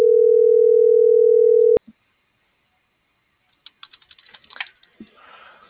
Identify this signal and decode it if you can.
Ringback tone (North America), 440 + 480 Hz.